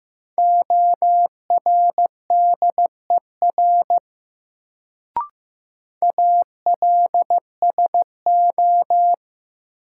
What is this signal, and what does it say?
Morse code, heard as a keyed tone, about 15 wpm, 700 Hz. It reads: ORDER ALSO
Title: Sentences from Top 400 Words - New Word Drill - 15wpm